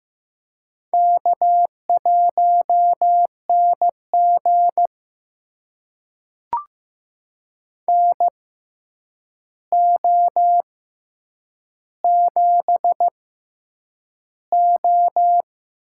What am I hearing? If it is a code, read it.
Morse code, heard as a keyed tone, about 15 wpm, spaced as for 8 wpm, 700 Hz. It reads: K1NG NO7O
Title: USA Call Signs - 4 Characters - 6x Character Spacing - 15wpm